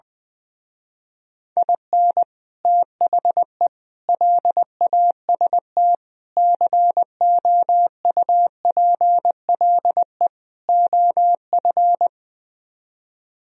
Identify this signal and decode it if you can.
Morse code, heard as a keyed tone, about 20 wpm, 700 Hz. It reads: IN THE LAST COUPLE OF
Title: Speed-Racing - Sets of 5 Words 20wpm